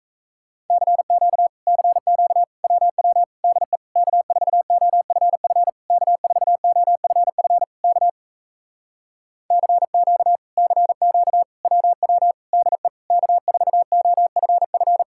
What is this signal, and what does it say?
Morse code, heard as a keyed tone, about 37 wpm, 700 Hz. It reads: CQ CQ WW DE K4OFF K4OFF K CQ CQ WW DE K4OFF